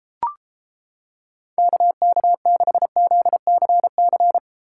Morse code, heard as a keyed tone, about 30 wpm, 700 Hz. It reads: KK6ZCC